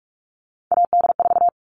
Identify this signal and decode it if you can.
Morse code, heard as a keyed tone, about 44 wpm, 700 Hz. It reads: AD4
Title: All Cut Numbers - 3-Digit 45wpm